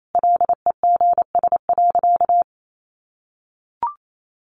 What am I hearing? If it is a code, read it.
Morse code, heard as a keyed tone, about 28 wpm, 700 Hz. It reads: LEGS.
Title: A Sound Of Thunder 28wpm